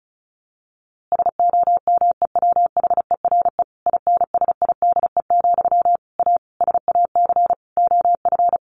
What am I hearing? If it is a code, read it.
Morse code, heard as a keyed tone, about 34 wpm, 700 Hz. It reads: SOMEWHERE INSIDE, A SAC OF
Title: A Sound Of Thunder 35wpm